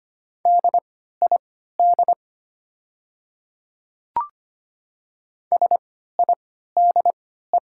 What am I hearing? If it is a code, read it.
Morse code, heard as a keyed tone, about 25 wpm, spaced as for 14 wpm, 700 Hz. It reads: DID SIDE
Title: 3x Character Spacing - Top 100 Words - 25wpm